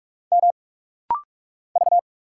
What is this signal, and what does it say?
Morse code, heard as a keyed tone, about 44 wpm, 690 Hz. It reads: M V
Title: Single Letter-Number - Rapid-Fire 45wpm